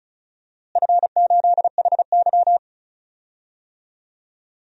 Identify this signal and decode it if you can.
Morse code, heard as a keyed tone, about 36 wpm, 700 Hz. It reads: F8HY
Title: DXCC-100 Call Signs - 4 Characters - Repeated 3x - 35wpm